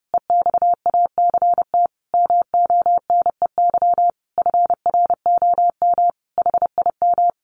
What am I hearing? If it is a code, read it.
Morse code, heard as a keyed tone, about 30 wpm, 700 Hz. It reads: EXACT MONEY FROM HIM
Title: Sentences from Top 1300 Words 30wpm